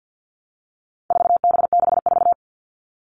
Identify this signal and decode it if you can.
Morse code, heard as a keyed tone, about 50 wpm, 700 Hz. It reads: 4B64
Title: All Cut Numbers - 4-Digit 50wpm